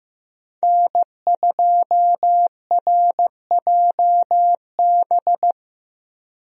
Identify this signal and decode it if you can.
Morse code, heard as a keyed tone, about 15 wpm, 700 Hz. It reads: N2RJB
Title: USA Call Signs - 5 Characters - Repeated 2x - 15wpm